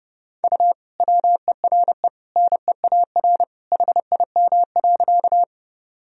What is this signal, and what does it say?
Morse code, heard as a keyed tone, about 30 wpm, 700 Hz. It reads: U WERE NEAR HIM.
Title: Sentences from Top 200 Words and Abbreviations 30wpm